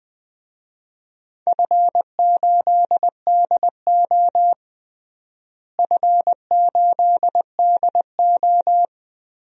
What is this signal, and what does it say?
Morse code, heard as a keyed tone, about 20 wpm, 700 Hz. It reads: F8DO F8DO
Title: DXCC-100 Call Signs - 4 Characters - Repeated 3x - 20wpm